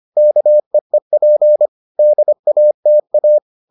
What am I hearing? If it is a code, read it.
Morse code, heard as a keyed tone, about 25 wpm, 600 Hz. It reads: KEEP DATA